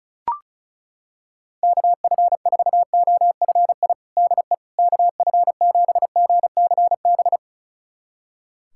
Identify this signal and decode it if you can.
Morse code, heard as a keyed tone, about 34 wpm, 700 Hz. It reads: KF4OFI DE KF7GCB
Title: USA Callsign DE Callsign - 6 Characters - 35wpm